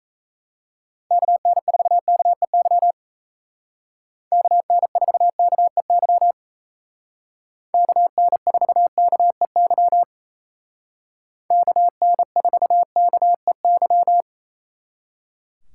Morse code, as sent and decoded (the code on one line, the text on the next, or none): -.- -. ....- -.- . -.-- / -.- -. ....- -.- . -.-- / -.- -. ....- -.- . -.-- / -.- -. ....- -.- . -.--
KN4KEY KN4KEY KN4KEY KN4KEY